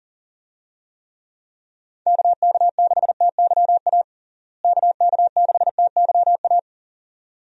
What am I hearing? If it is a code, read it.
Morse code, heard as a keyed tone, about 40 wpm, 700 Hz. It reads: KK6TYA KK6TYA
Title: USA Call Signs - 6 Characters - Repeated 2x - 40wpm